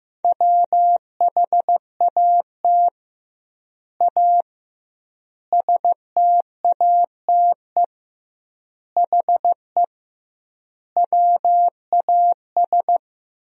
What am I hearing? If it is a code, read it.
Morse code, heard as a keyed tone, about 15 wpm, 700 Hz. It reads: WHAT A STATE HE WAS
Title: 2x Word-Spacing - Sentences from Top 400 Words - 15wpm